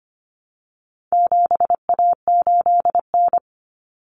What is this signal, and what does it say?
Morse code, heard as a keyed tone, about 25 wpm, 700 Hz. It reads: 7A8N
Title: Common Cut Numbers - 4-Digit 25wpm